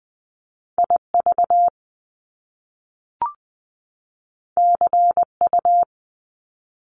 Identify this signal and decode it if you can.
Morse code, heard as a keyed tone, about 20 wpm, 700 Hz. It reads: I V C U
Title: Top Two Letter Combinations in Words 20wpm